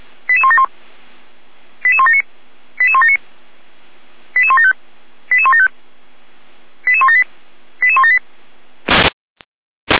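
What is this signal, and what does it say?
Five-tone selective call (ZVEI), heard as quick runs of five tones, tones of about 70 ms. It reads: 88171 88179 88179 88175 88175 88178 88178
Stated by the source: ZVEI five-tone selective call 88352 received by SDR, from the feuerwehr-orlamuende.org fire brigade server.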